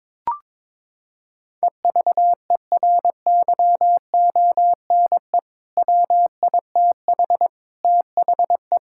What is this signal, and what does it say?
Morse code, heard as a keyed tone, about 22 wpm, 700 Hz. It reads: EVERYONE WITH THE
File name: practice / 22wpm / Sentences with 4-Word Phrases - 22wpm